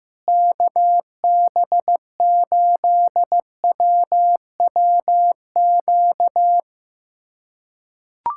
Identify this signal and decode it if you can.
Morse code, heard as a keyed tone, about 15 wpm, 700 Hz. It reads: KB8WWQ
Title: USA Callsign DE Callsign - 6 Characters - 15wpm